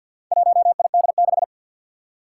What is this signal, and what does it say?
Morse code, heard as a keyed tone, about 51 wpm, 700 Hz. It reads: 1ID6
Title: Four Challenging Letter-Number Combinations 50wpm